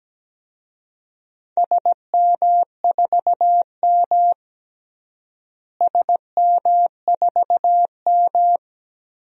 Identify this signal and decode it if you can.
Morse code, heard as a keyed tone, about 17 wpm, 700 Hz. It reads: SM4M SM4M
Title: DXCC-100 Call Signs - 4 Characters - Repeated 3x - 17wpm